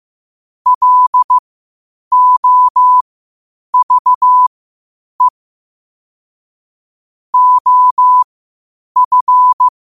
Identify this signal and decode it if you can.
Morse code, heard as a keyed tone, about 15 wpm, spaced as for 8 wpm, 1000 Hz. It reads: LOVE OF
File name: qotdfiles / 8wpm